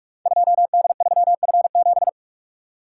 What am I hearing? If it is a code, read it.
Morse code, heard as a keyed tone, about 44 wpm, 700 Hz. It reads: 2D3F7
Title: Code Groups - 5 Letter:Numbers - 45wpm